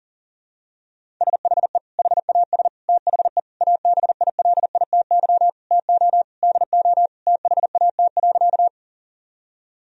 Morse code, heard as a keyed tone, about 40 wpm, 700 Hz. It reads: SHE HAS THE ABILITY TO DO THAT.
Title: Sentences from Top 1000 Words - Review Part 3 - 40wpm